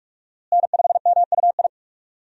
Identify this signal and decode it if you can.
Morse code, heard as a keyed tone, about 46 wpm, 700 Hz. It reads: N5MUI
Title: USA Call Signs - 5 Characters - Repeated 2x - 45wpm